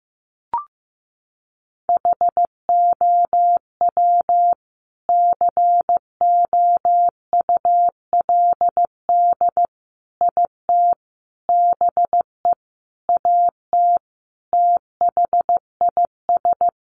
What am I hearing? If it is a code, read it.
Morse code, heard as a keyed tone, about 15 wpm, 700 Hz. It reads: HOW COULD IT BE AT THIS
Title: Sentences from Top 1400 Words 15wpm